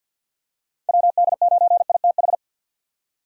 Morse code, heard as a keyed tone, about 47 wpm, 700 Hz. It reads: WD9ITH
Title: USA Call Signs - 4x Character Spacing - 50wpm